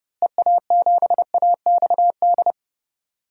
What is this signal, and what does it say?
Morse code, heard as a keyed tone, about 30 wpm, 700 Hz. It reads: EA7AXD
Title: DXCC-100 Call Signs - 6 Characters - Repeated 3x - 30wpm